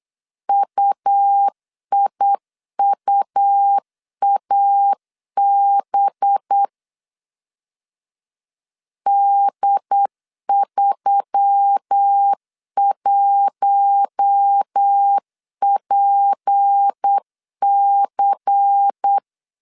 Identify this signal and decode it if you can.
Morse code, heard as a keyed tone, about 8 wpm, 800 Hz. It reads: UIUAB D31PC
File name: CW / DET / 1 6 1500C S B 8wpm 5142